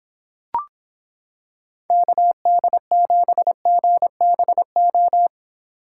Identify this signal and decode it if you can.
Morse code, heard as a keyed tone, about 26 wpm, 700 Hz. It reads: KD7GBO